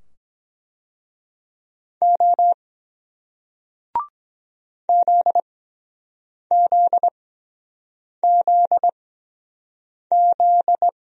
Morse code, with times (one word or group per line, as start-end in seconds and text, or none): 2.0-2.5 O
4.9-10.9 ZZZZ